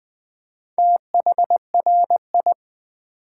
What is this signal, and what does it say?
Morse code, heard as a keyed tone, about 20 wpm, 700 Hz. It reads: THRI
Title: Build A Word - 4 Letters At A Time - 8-Letter Words - 20wpm